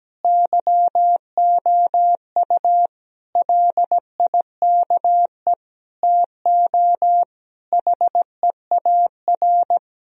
Morse code, heard as a keyed tone, about 17 wpm, 700 Hz. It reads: YOU LIKE TO HEAR
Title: Sentences from Top 1000 Words - Review Part 1 - 17wpm